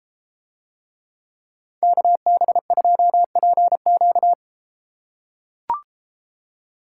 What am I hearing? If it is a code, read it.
Morse code, heard as a keyed tone, about 32 wpm, 700 Hz. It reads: KB2PQ